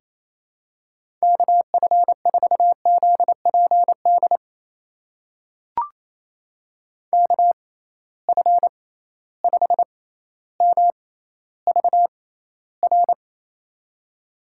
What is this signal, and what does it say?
Morse code, heard as a keyed tone, about 27 wpm, spaced as for 14 wpm, 700 Hz. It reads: KF4ZPD KF5MVR
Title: USA Call Signs - 6 Characters - 6x Character Spacing - 28wpm